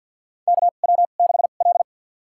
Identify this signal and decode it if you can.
Morse code, heard as a keyed tone, about 52 wpm, spaced as for 37 wpm, 690 Hz. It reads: KW6L
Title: USA Call Signs - 2x Character Spacing - 50wpm